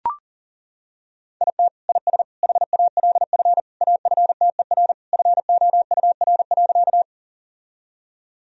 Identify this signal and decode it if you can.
Morse code, heard as a keyed tone, about 40 wpm, 700 Hz. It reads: IT IS HALF AFTER FOUR.
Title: Sentences from Top 1000 Words - Review Part 1 - 40wpm